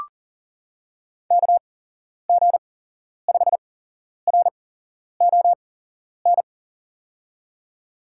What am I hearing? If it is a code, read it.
Morse code, heard as a keyed tone, about 40 wpm, spaced as for 11 wpm, 700 Hz. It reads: KG5RON
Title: USA Call Signs - 6 Characters - 8x Character Spacing - 40wpm